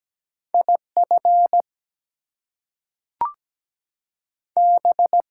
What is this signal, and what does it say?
Morse code, heard as a keyed tone, about 17 wpm, 700 Hz. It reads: IF B
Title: Two Letter Words - 17wpm